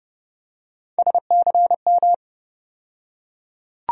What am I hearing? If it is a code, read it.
Morse code, heard as a keyed tone, about 30 wpm, 700 Hz. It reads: SCM
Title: USA Call Sign Postfixes - 3 Letters 30wpm